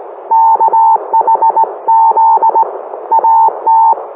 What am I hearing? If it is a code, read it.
Morse code, heard as a keyed tone, about 18 wpm, 900 Hz. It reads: KHZ AT